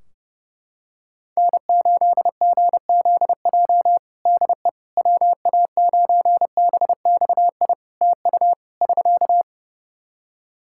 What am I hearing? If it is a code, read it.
Morse code, heard as a keyed tone, about 30 wpm, 700 Hz. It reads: N8GZJ DE WA9BXI TU <SK>
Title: K1USN SST QSO Practice 30wpm